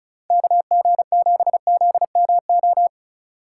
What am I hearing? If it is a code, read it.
Morse code, heard as a keyed tone, about 36 wpm, 700 Hz. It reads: KG7ZMO
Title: USA Call Signs - 6 Characters - 4x Character Spacing - 35wpm